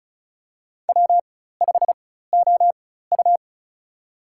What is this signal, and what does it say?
Morse code, heard as a keyed tone, about 34 wpm, spaced as for 16 wpm, 700 Hz. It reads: W5OU
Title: USA Call Signs - 4 Characters - 4x Character Spacing - 35wpm